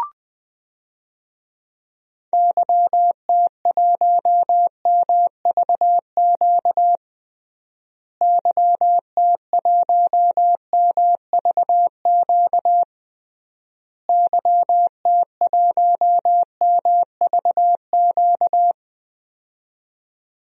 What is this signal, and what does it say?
Morse code, heard as a keyed tone, about 20 wpm, 700 Hz. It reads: YT1MVQ YT1MVQ YT1MVQ
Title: DXCC-100 Call Signs - 6 Characters - Repeated 3x - 20wpm